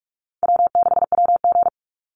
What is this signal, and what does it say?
Morse code, heard as a keyed tone, about 44 wpm, 700 Hz. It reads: W6WG
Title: Easy Call Signs - 45wpm